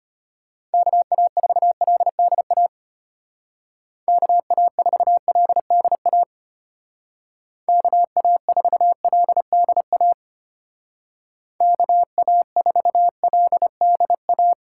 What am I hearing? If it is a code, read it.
Morse code, heard as a keyed tone, about 33 wpm, 700 Hz. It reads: KA4LDA KA4LDA KA4LDA KA4LDA